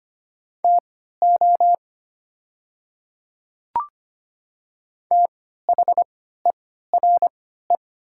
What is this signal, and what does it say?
Morse code, heard as a keyed tone, about 25 wpm, spaced as for 14 wpm, 700 Hz. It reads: TO THERE
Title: 3x Character Spacing - Top 100 Words - 25wpm